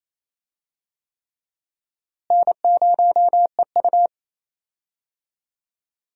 Morse code, heard as a keyed tone, about 28 wpm, 700 Hz. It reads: N0EU
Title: USA Call Signs - 4 Characters - Repeated 3x - 28wpm